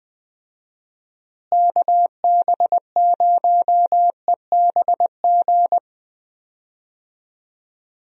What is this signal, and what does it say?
Morse code, heard as a keyed tone, about 21 wpm, 700 Hz. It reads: KB0EBG
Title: USA Call Signs - 6 Characters - Repeated 2x - 20wpm